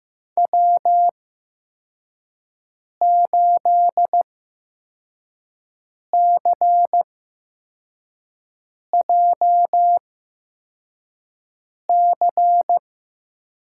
Morse code, heard as a keyed tone, about 15 wpm, spaced as for 4 wpm, 700 Hz. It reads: W8CJC
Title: USA Call Signs - 8x Character Spacing - 15wpm